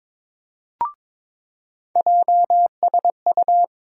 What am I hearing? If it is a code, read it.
Morse code, heard as a keyed tone, about 22 wpm, 700 Hz. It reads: JSU
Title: USA Call Sign Postfixes - 3 Letters 22wpm